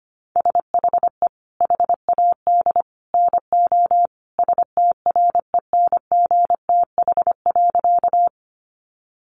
Morse code, heard as a keyed tone, about 25 wpm, 700 Hz. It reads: SHE HAD NO STRENGTH.
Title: Sentences from Top 1300 Words 25wpm